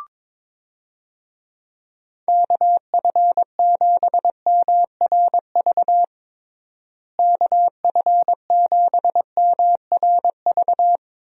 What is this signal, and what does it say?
Morse code, heard as a keyed tone, about 22 wpm, 700 Hz. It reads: KF7MRV KF7MRV
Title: USA Call Signs - 6 Characters - Repeated 2x - 22wpm